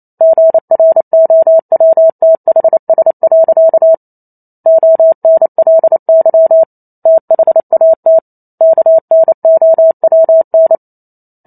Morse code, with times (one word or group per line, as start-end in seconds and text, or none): 0.2-3.9 GROWTHS.
4.7-6.6 ONLY
7.1-8.2 THAT
8.6-10.8 KNOWN